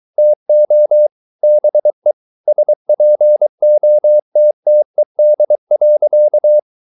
Morse code, heard as a keyed tone, about 23 wpm, 600 Hz. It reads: TO BE SPOTTED.